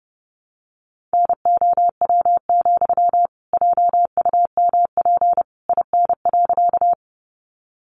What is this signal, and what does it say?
Morse code, heard as a keyed tone, about 30 wpm, 700 Hz. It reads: NOW, JUMP IN.
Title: Sentences from Top 1300 Words 30wpm